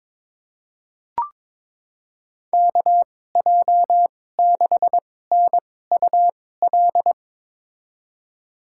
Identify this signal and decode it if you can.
Morse code, heard as a keyed tone, about 21 wpm, spaced as for 16 wpm, 700 Hz. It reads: KJ6NUL
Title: USA Call Signs - 6 Characters - 2x Character Spacing - 22wpm